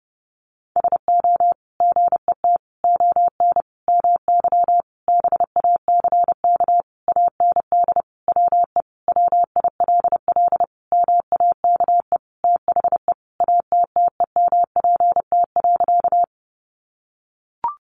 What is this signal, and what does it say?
Morse code, heard as a keyed tone, about 30 wpm, 700 Hz. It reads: SO GET ON MY BACK AND WE WILL MAKE THE ATTEMPT.
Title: Sentences from Top 1000 Words 30wpm